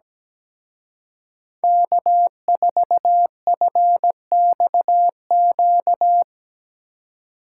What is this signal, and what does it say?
Morse code, heard as a keyed tone, about 20 wpm, 700 Hz. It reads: K4FXQ
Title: Speed Racing - USA 1x3 Call Signs 17wpm